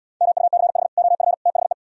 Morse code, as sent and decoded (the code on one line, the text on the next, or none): -- ..- -.-. .... / --- ..-. / - .... .
MUCH OF THE